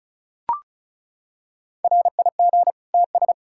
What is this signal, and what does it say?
Morse code, heard as a keyed tone, about 36 wpm, 700 Hz. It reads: RIG TS
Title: Practice QSO - Exchange 2 - Info - 35wpm